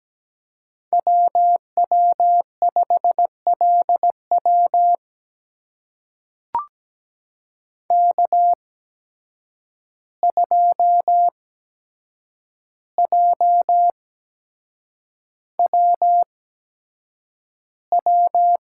Morse code, heard as a keyed tone, about 17 wpm, spaced as for 7 wpm, 700 Hz. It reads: WW5LW K2JWW
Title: USA Call Signs - 5 Characters - 8x Character Spacing - 17wpm